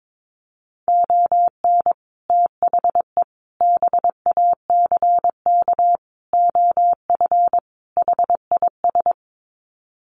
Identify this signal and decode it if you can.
Morse code, heard as a keyed tone, about 22 wpm, 700 Hz. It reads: ON THE BACK OF HIS
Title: Sets of 5 Words Encore 22wpm